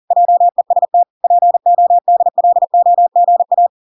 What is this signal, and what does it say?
Morse code, heard as a keyed tone, about 40 wpm, 700 Hz. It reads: JEST PODLOGA